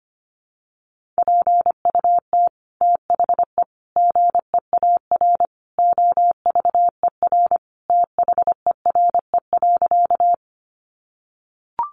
Morse code, heard as a keyed tone, about 25 wpm, 700 Hz. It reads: PUT THE GEAR OVER THERE.
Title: Sentences from Top 1400 Words 25wpm